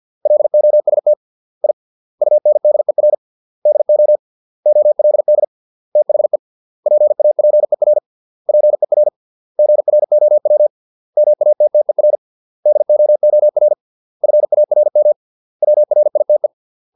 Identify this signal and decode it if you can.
Morse code, heard as a keyed tone, about 50 wpm, 600 Hz. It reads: LOST I UNDER DO OLD THE PAPER PER GROW MATTER DOOR FARM WRITE